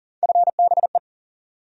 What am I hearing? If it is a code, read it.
Morse code, heard as a keyed tone, about 40 wpm, 700 Hz. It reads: FBE
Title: USA Call Sign Postfixes - 3 Letters 40wpm